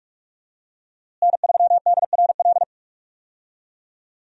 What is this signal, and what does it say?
Morse code, heard as a keyed tone, about 41 wpm, 700 Hz. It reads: N3DRL
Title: USA Call Signs - 5 Characters - Repeated 2x - 45wpm